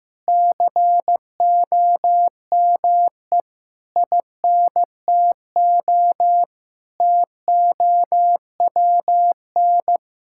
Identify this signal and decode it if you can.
Morse code, heard as a keyed tone, about 15 wpm, 700 Hz. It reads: COME INTO TOWN
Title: Sentences from Top 1000 Words - Review Part 5 - 15wpm